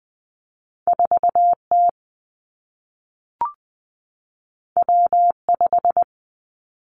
Morse code, heard as a keyed tone, about 20 wpm, 700 Hz. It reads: 4 T W 5
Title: Two Challenging Letter-Number Combinations 20wpm